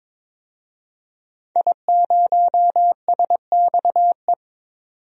Morse code, heard as a keyed tone, about 22 wpm, 700 Hz. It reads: I0SXE
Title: DXCC-100 Call Signs - 5 Characters - 22wpm